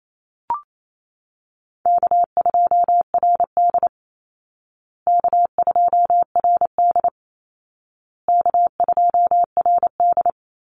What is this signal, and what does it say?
Morse code, heard as a keyed tone, about 28 wpm, 700 Hz. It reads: K2RD K2RD K2RD
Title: Easy Call Signs Repeated 3x - 28wpm